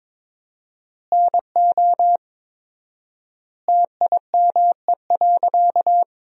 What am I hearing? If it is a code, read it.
Morse code, heard as a keyed tone, about 22 wpm, 700 Hz. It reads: NO TIME.
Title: 4x Word-Spacing - Sentences from Top 100 Words - 22wpm